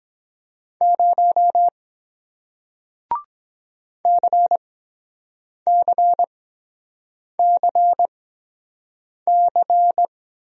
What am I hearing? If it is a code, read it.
Morse code, heard as a keyed tone, about 23 wpm, spaced as for 7 wpm, 700 Hz. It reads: 0 CCCC